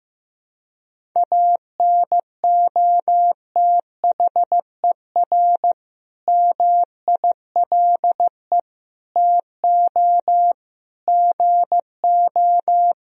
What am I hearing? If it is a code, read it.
Morse code, heard as a keyed tone, about 15 wpm, 700 Hz. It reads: ANOTHER MILE TO GO
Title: Sentences from Top 1000 Words - Review Part 1 - 15wpm